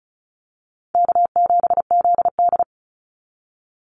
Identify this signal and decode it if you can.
Morse code, heard as a keyed tone, about 34 wpm, 700 Hz. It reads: K7ZD
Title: Easy Call Signs Repeated 2x - 35wpm